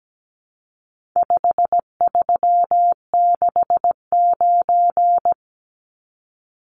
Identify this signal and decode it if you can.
Morse code, heard as a keyed tone, about 17 wpm, 700 Hz. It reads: 5369
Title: Numbers - 4-Digit 17wpm